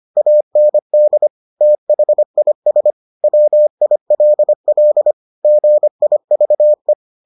Morse code, heard as a keyed tone, about 25 wpm, 600 Hz. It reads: AND THIS WILL GIVE